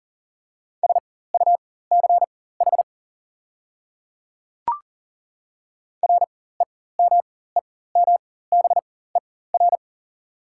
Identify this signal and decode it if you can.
Morse code, heard as a keyed tone, about 40 wpm, spaced as for 19 wpm, 700 Hz. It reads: SUCH REMEMBER
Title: 4x Character Spacing - Top 500 Words - 40wpm